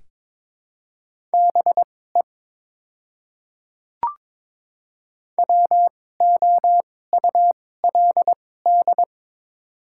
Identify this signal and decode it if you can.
Morse code, heard as a keyed tone, about 21 wpm, spaced as for 16 wpm, 700 Hz. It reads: BE WOULD
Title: 2x Character Spacing - Top 100 Words - 22wpm